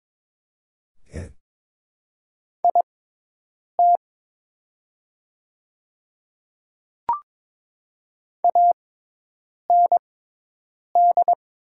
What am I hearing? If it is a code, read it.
Morse code, heard as a keyed tone, about 22 wpm, spaced as for 8 wpm, 700 Hz. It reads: IT AND
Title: 6x Character Spacing - Top 300 Words - 22wpm